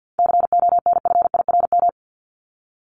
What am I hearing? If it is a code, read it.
Morse code, heard as a keyed tone, about 50 wpm, 700 Hz. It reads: CONFIRM
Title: Top 1751-2000 Words 50wpm